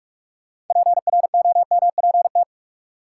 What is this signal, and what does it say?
Morse code, heard as a keyed tone, about 43 wpm, 700 Hz. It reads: PROMPT
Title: Six Letter Words - 45wpm